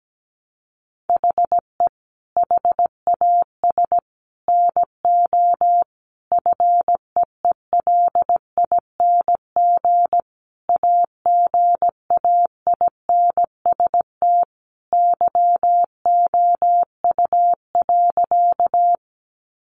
Morse code, heard as a keyed tone, about 17 wpm, 700 Hz. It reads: HE HAS NO FEELING AGAINST YOU.
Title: Sentences from Top 1200 Words 17wpm